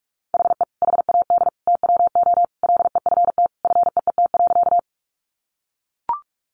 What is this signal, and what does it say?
Morse code, heard as a keyed tone, about 44 wpm, 700 Hz. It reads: HE HAD TWO LEFT FEET.
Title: Sentences from Top 300 Words 45wpm